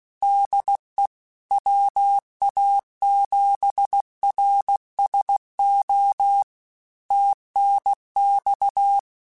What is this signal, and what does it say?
Morse code, heard as a keyed tone, about 16 wpm, 800 Hz. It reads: DE WA7RSO TNX
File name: QSOs / QSOs-mp3s / kc7haa@15